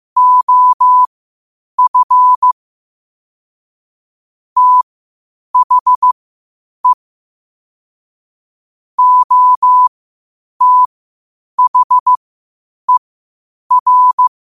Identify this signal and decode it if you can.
Morse code, heard as a keyed tone, about 15 wpm, spaced as for 8 wpm, 1000 Hz. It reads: OF THE OTHER
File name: qotdfiles / 8wpm